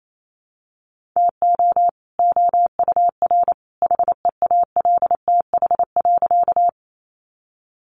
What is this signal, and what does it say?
Morse code, heard as a keyed tone, about 28 wpm, 700 Hz. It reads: TO OUR HEALTH.
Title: Sentences from Top 600 Words 28wpm